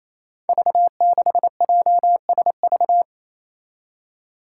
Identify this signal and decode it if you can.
Morse code, heard as a keyed tone, about 28 wpm, 700 Hz. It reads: V6JSV
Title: Five Challenging Letter-Number Combinations 28wpm